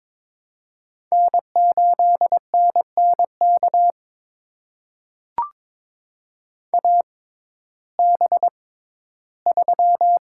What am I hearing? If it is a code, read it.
Morse code, heard as a keyed tone, about 23 wpm, spaced as for 14 wpm, 700 Hz. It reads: N8NNK AB3
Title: USA Call Signs - 5 Characters - 6x Character Spacing - 22wpm